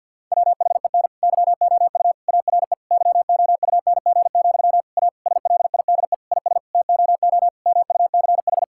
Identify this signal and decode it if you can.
Morse code, heard as a keyed tone, about 50 wpm, 700 Hz. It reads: WHEN YOU ARE YOUNG, A SLIDE IS TOO MUCH
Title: Sentences from Top 1600 Words 50wpm